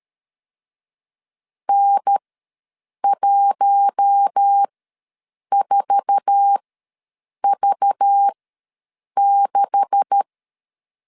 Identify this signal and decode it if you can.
Morse code, heard as a keyed tone, about 13 wpm, spaced as for 7 wpm, 800 Hz. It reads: N14V6
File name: LEZIONI / 9 286 ws 12wpm 3351